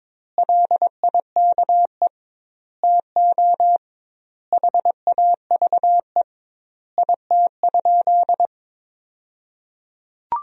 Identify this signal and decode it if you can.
Morse code, heard as a keyed tone, about 22 wpm, 700 Hz. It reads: LIKE TO HAVE IT?
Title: 2x Word-Spacing - Sentences from Top 100 Words - 22wpm